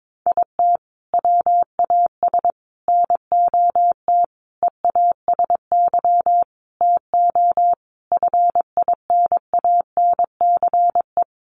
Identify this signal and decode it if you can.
Morse code, heard as a keyed tone, about 22 wpm, 700 Hz. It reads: IT WAS NOT EASY TO FINANCE
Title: Sentences from Top 1500 Words 22wpm